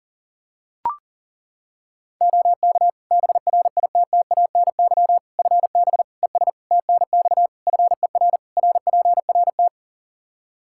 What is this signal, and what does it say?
Morse code, heard as a keyed tone, about 40 wpm, 700 Hz. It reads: OK BRITTANY FB ES TNX FER RPRT
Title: K1USN SST QSO Practice 40wpm